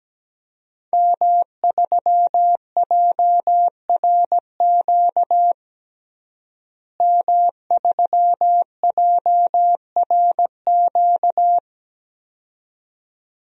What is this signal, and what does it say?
Morse code, heard as a keyed tone, about 17 wpm, 700 Hz. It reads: M3JRQ M3JRQ
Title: DXCC-100 Call Signs - 5 Characters - Repeated 2x - 17wpm